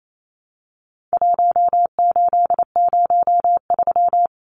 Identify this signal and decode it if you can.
Morse code, heard as a keyed tone, about 28 wpm, 700 Hz. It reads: 1803
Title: Numbers - 4-Digit 28wpm